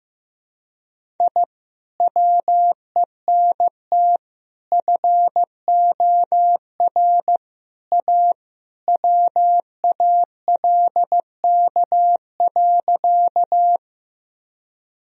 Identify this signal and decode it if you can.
Morse code, heard as a keyed tone, about 16 wpm, 700 Hz. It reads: I WENT FOR A WALK.
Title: Sentences from Top 1000 Words - Review Part 1 - 15wpm